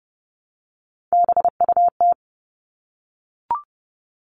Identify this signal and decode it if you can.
Morse code, heard as a keyed tone, about 30 wpm, 700 Hz. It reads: BUT
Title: Top 500 Words 30wpm